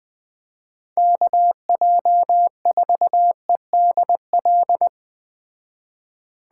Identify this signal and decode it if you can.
Morse code, heard as a keyed tone, about 19 wpm, 700 Hz. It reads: KJ4EDL
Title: Speed Racing - USA 2x3 Call Signs 17wpm